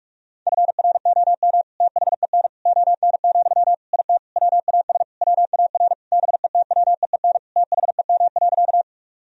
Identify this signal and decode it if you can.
Morse code, heard as a keyed tone, about 46 wpm, 700 Hz. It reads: FROM THEN ON, IT WAS WAR BETWEEN THEM.
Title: Sentences from Top 1000 Words - Review Part 2 - 45wpm